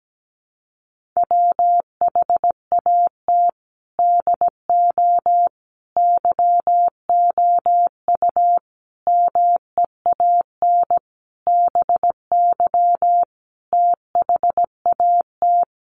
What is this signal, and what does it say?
Morse code, heard as a keyed tone, about 17 wpm, 700 Hz. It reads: WHAT DO YOU MEAN BY THAT
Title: Sentences from Top 400 Words 17wpm